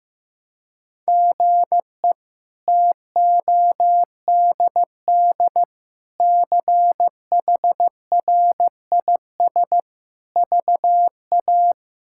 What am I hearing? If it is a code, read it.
Morse code, heard as a keyed tone, about 15 wpm, 700 Hz. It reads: GE TODD CHRIS VA
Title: K1USN SST QSO Practice - 15wpm